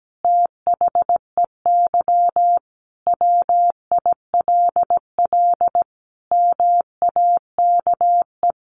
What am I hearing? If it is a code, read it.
Morse code, heard as a keyed tone, about 17 wpm, 700 Hz. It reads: THEY WILL MAKE
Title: Sentences from Top 1500 Words 17wpm